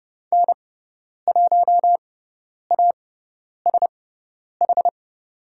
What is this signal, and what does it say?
Morse code, heard as a keyed tone, about 30 wpm, spaced as for 10 wpm, 700 Hz. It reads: N1ASH
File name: Lesson 17g - Letter u - Call Signs 30wpm